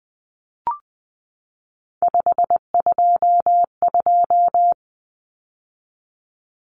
Morse code, heard as a keyed tone, about 20 wpm, 700 Hz. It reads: 522
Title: Numbers - 3-Digit 20wpm